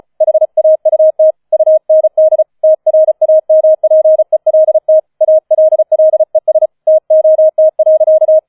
Morse code, heard as a keyed tone, about 34 wpm, 630 Hz. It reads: HAUT UND TRAMPELT ALLES TOT.
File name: 005 Fünfter Streich 35wpm